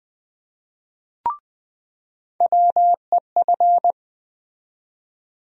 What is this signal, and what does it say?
Morse code, heard as a keyed tone, about 20 wpm, 700 Hz. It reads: WEF
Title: USA Call Sign Postfixes - 3 Letters 20wpm